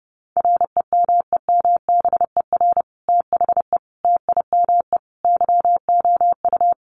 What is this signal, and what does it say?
Morse code, heard as a keyed tone, about 30 wpm, 700 Hz. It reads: REMEMBER THE TIME YOU
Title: Sentences from Top 1400 Words 30wpm